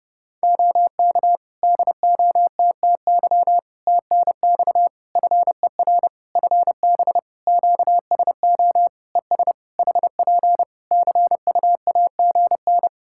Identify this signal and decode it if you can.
Morse code, heard as a keyed tone, about 30 wpm, 700 Hz. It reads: OK DOTTY TNX FER FB QSO ES HP CUAGN
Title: Practice QSO - Exchange 4 - Ending - 30wpm